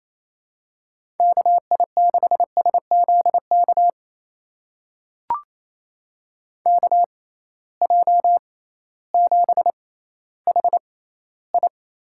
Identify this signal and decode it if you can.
Morse code, heard as a keyed tone, about 28 wpm, spaced as for 15 wpm, 700 Hz. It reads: KI6SZK KJ7HI
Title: USA Call Signs - 6 Characters - 6x Character Spacing - 28wpm